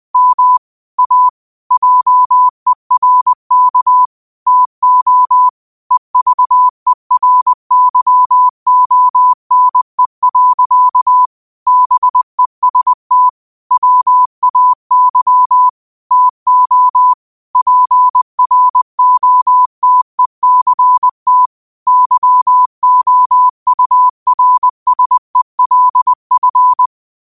Morse code, heard as a keyed tone, about 20 wpm, 1000 Hz. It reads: M A JERK TO EVERYONE. BEST WAY TO PROTECT YOURSELF
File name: qotdfiles / 20wpm